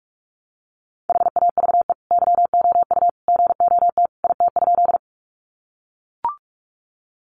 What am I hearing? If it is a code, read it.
Morse code, heard as a keyed tone, about 44 wpm, 700 Hz. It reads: HAVE YOU GOT IT?
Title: Sentences from Top 300 Words 45wpm